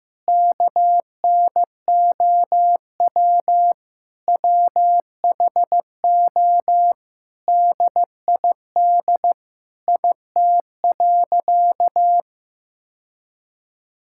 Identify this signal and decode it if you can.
Morse code, heard as a keyed tone, about 15 wpm, 700 Hz. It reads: KNOW WHO DID IT.
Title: Sentences from Top 1000 Words - Review Part 6 - 15wpm